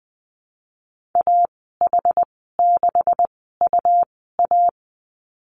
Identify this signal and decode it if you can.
Morse code, heard as a keyed tone, about 20 wpm, spaced as for 14 wpm, 700 Hz. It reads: AH6UA
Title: 231 Lesson 37g - Letter x - Call Signs 20wpm